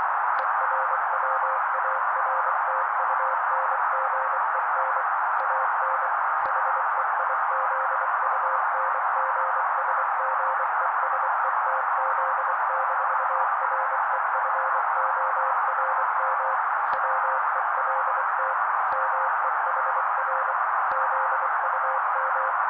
Morse code, heard as a keyed tone, about 24 wpm, 530 Hz. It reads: ERWARTUNGEN AN HEIZUNGSGESETZ-REFORM WELT MEHR ZUM
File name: daily / 23wpm noise